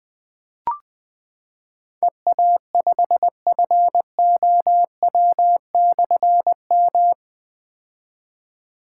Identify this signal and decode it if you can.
Morse code, heard as a keyed tone, about 20 wpm, 700 Hz. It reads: EA5FOW/M
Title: Difficult Call Signs 20wpm